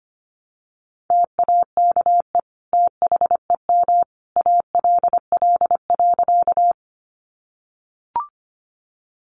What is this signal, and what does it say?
Morse code, heard as a keyed tone, about 25 wpm, 700 Hz. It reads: TAKE THEM ALL.
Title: Sentences from Top 200 Words 25wpm